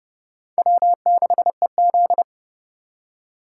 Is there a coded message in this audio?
Yes: Morse code from 0.6 to 2.2 s, heard as a keyed tone, about 38 wpm, 700 Hz. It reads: W6EZ